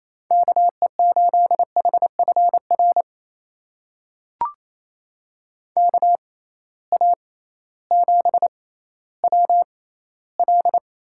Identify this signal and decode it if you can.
Morse code, heard as a keyed tone, about 28 wpm, spaced as for 15 wpm, 700 Hz. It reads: KE8HFR KA7WL
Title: USA Call Signs - 6 Characters - 6x Character Spacing - 28wpm